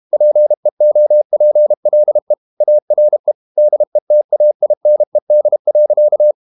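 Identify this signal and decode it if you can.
Morse code, heard as a keyed tone, about 32 wpm, 600 Hz. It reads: PEOPLE ARE DETAINED.
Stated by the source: Morse code practice at 32 wpm from CW med Gnister, reading Ken Follett's novel Never in English.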